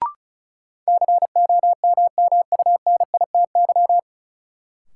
Morse code, heard as a keyed tone, about 36 wpm, 700 Hz. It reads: COMMUNITY
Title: Sentences from Top 1000 Words - Review Part 4 - New Word Drill - 35wpm